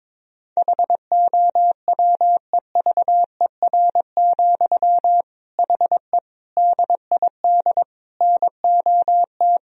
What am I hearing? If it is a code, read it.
Morse code, heard as a keyed tone, about 22 wpm, 700 Hz. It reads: HOWEVER, HE DID NOT
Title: Sentences from Top 1000 Words - Review Part 3 - 22wpm